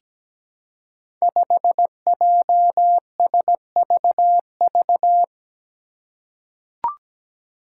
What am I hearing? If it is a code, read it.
Morse code, heard as a keyed tone, about 17 wpm, 700 Hz. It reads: 5JSVV
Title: Five Challenging Letter-Number Combinations 17wpm